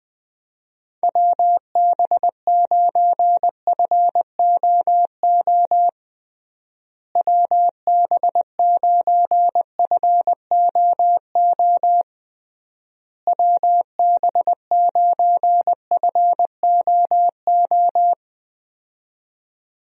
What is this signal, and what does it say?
Morse code, heard as a keyed tone, about 21 wpm, 700 Hz. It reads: WB9FOO WB9FOO WB9FOO
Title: USA Call Signs - 6 Characters - Repeated 3x - 20wpm